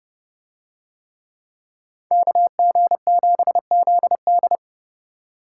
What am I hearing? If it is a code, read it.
Morse code, heard as a keyed tone, about 29 wpm, 700 Hz. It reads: KG7ZD